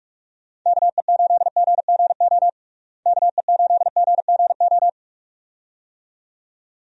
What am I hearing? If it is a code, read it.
Morse code, heard as a keyed tone, about 42 wpm, 700 Hz. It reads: KE8GGO KE8GGO
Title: USA Call Signs - 6 Characters - Repeated 2x - 45wpm